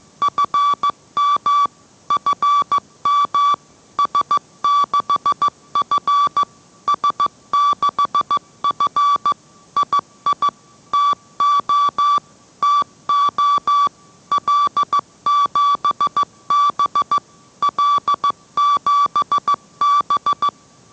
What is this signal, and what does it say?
Morse code, heard as a keyed tone, about 15 wpm, 1150 Hz. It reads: FM FM S6F S6F II TO TO L7B L7B